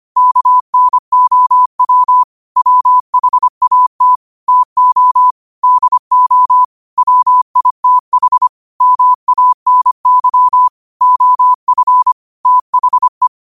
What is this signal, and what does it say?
Morse code, heard as a keyed tone, about 25 wpm, 1000 Hz. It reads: KNOW WHAT TO DO WITH MANY OF THE
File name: qotdfiles / 25wpm